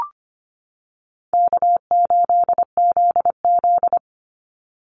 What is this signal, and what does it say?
Morse code, heard as a keyed tone, about 25 wpm, 700 Hz. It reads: K8ZZ
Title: Easy Call Signs - 25wpm